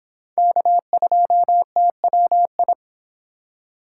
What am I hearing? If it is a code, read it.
Morse code, heard as a keyed tone, about 26 wpm, 700 Hz. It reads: K2TWI